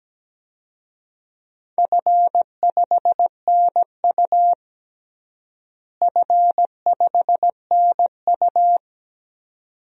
Morse code, heard as a keyed tone, about 17 wpm, 700 Hz. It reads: F5NU F5NU
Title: DXCC-100 Call Signs - 4 Characters - Repeated 2x - 17wpm